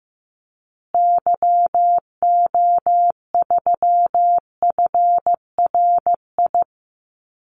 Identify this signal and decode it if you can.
Morse code, heard as a keyed tone, about 15 wpm, 700 Hz. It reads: YO3FRI
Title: Call Signs Repeated 2x - 15wpm